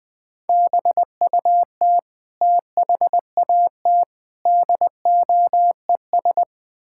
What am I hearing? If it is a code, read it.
Morse code, heard as a keyed tone, about 20 wpm, 700 Hz. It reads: BUT THAT DOES
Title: Build A Phrase - 5-Word Phrases - 20wpm